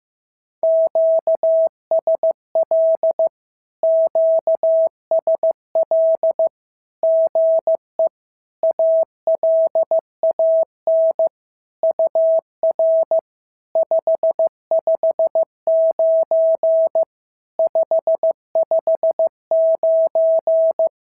Morse code, heard as a keyed tone, about 15 wpm, 660 Hz. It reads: QSL QSL GE ALAN UR 559 559
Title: SOTA QSO Practice 15wpm